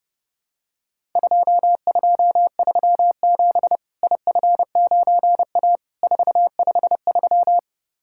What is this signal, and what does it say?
Morse code, heard as a keyed tone, about 30 wpm, 700 Hz. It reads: 2237 IF9A 453
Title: Worked All Europe DX Contest - QTC Exchange - 30wpm